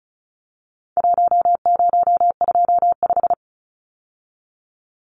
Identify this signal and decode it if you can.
Morse code, heard as a keyed tone, about 34 wpm, 700 Hz. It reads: 1025
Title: Numbers - Contest Style - 4-Digit 35wpm